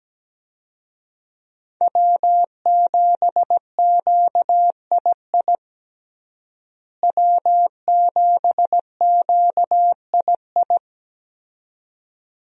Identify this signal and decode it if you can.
Morse code, heard as a keyed tone, about 17 wpm, 700 Hz. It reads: W7QII W7QII
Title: USA Call Signs - 5 Characters - Repeated 2x - 17wpm